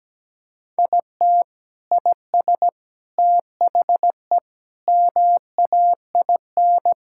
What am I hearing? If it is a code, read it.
Morse code, heard as a keyed tone, about 17 wpm, 700 Hz. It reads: IT IS THE MAIN
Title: Sentences from Top 1000 Words - Review Part 5 - 17wpm